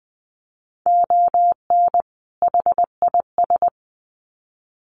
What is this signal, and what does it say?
Morse code, heard as a keyed tone, about 20 wpm, 700 Hz. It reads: ON HIS
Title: Sets of 2 Words 20wpm